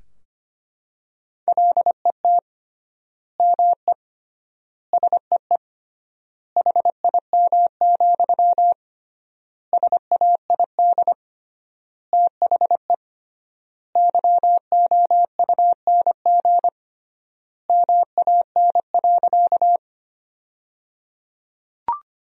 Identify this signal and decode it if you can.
Morse code, heard as a keyed tone, about 25 wpm, 700 Hz. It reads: LET ME SEE HIM, SAID THE YOUNG MAN.
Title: 3x Word-Spacing - Sentences from Top 400 Words - 25wpm